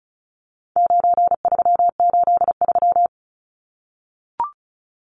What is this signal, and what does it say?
Morse code, heard as a keyed tone, about 34 wpm, 700 Hz. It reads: 9383
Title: Numbers - 4-Digit 35wpm